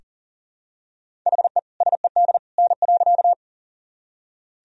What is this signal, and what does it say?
Morse code, heard as a keyed tone, about 40 wpm, 700 Hz. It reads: HE SED N.
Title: Sentences from Top 100 Words and Abbreviations 40wpm